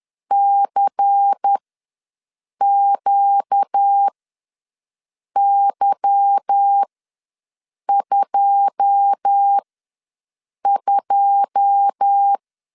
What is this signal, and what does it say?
Morse code, heard as a keyed tone, about 10 wpm, spaced as for 6 wpm, 800 Hz. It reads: CQY22
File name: LEZIONI / 9 2 ws 10wpm 4130